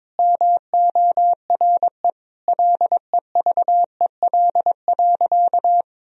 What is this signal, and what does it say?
Morse code, heard as a keyed tone, about 22 wpm, 700 Hz. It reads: MORE LEVEL.
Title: Sentences from Top 1000 Words - Review Part 4 - 22wpm